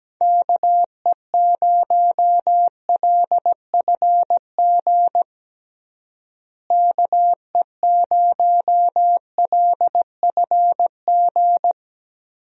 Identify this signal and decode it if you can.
Morse code, heard as a keyed tone, about 17 wpm, 700 Hz. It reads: KE0LFG KE0LFG
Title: USA Call Signs - 6 Characters - Repeated 2x - 17wpm